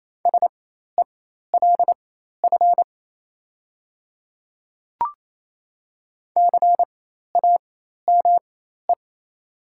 Morse code, heard as a keyed tone, about 28 wpm, spaced as for 13 wpm, 700 Hz. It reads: SELF CAME
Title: 4x Character Spacing - Top 500 Words - 28wpm